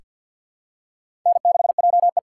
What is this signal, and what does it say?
Morse code, heard as a keyed tone, about 47 wpm, 700 Hz. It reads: N6JE
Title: USA Call Signs - 4 Characters - 3x Character Spacing - 50wpm